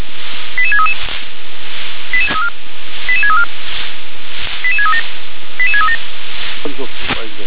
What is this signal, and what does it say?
Five-tone selective call (ZVEI), heard as quick runs of five tones, tones of about 70 ms, with speech. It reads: 88633 88634 88637 88637